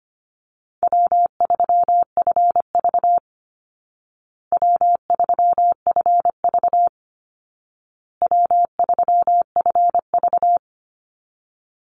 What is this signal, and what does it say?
Morse code, heard as a keyed tone, about 25 wpm, 700 Hz. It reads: W3FV W3FV W3FV
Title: Easy Call Signs Repeated 3x - 25wpm